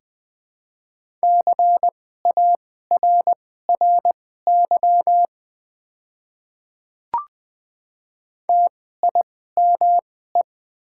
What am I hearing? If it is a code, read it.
Morse code, heard as a keyed tone, about 20 wpm, spaced as for 15 wpm, 700 Hz. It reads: CARRY TIME
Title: 2x Character Spacing - Top 500 Words - 20wpm